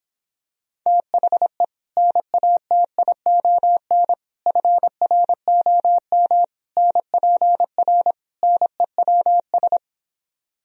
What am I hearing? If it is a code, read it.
Morse code, heard as a keyed tone, about 26 wpm, 700 Hz. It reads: THE NATION FROM NPR NEWS
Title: Speed-Racing - Sets of 5 Words 22wpm